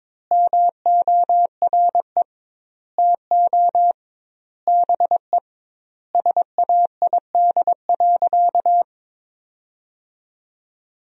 Morse code, heard as a keyed tone, about 22 wpm, 700 Hz. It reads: MORE TO BE SAID.
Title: 2x Word-Spacing - Sentences from Top 100 Words - 22wpm